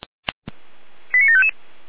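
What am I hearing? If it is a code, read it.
Five-tone selective call (ZVEI), heard as quick runs of five tones, tones of about 80 ms. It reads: 89755
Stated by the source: ZVEI five-tone selective call 89490 received on SDR.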